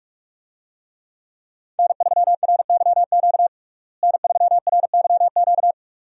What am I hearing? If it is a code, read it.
Morse code, heard as a keyed tone, about 46 wpm, 700 Hz. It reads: N3RYQ N3RYQ
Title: USA Call Signs - 5 Characters - Repeated 2x - 45wpm